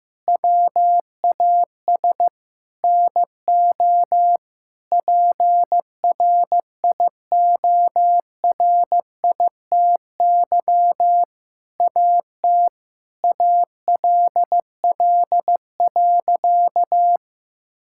Morse code, heard as a keyed tone, about 15 wpm, 700 Hz. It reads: WAS NO PRIORITY AT ALL.
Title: Sentences from Top 1600 Words 15wpm